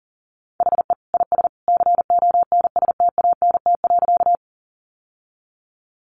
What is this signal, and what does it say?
Morse code, heard as a keyed tone, about 40 wpm, 700 Hz. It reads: HE IS CONSTANT.
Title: Sentences from Top 1500 Words 40wpm